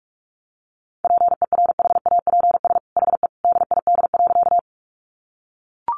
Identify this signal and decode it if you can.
Morse code, heard as a keyed tone, about 44 wpm, 700 Hz. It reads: PERHAPS HE DID.
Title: Sentences from Top 500 Words 45wpm